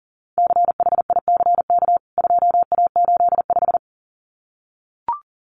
Morse code, heard as a keyed tone, about 40 wpm, 700 Hz. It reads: CHICK 2A85
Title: CWT Exchange 40wpm